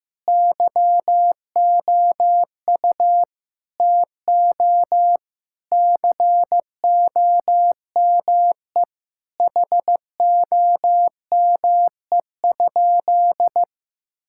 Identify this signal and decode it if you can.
Morse code, heard as a keyed tone, about 15 wpm, 700 Hz. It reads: YOU TO COME HOME?
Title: Sentences from Top 1600 Words 15wpm